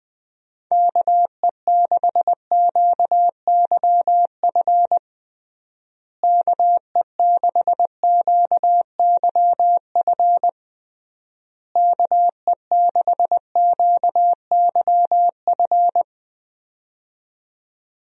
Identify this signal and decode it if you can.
Morse code, heard as a keyed tone, about 20 wpm, 700 Hz. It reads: KE6QYF KE6QYF KE6QYF
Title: USA Call Signs - 6 Characters - Repeated 3x - 20wpm